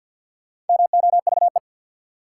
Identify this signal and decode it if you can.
Morse code, heard as a keyed tone, about 52 wpm, 700 Hz. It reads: MOVE
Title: Sentences from Top 200 Words - New Word Drill - 50wpm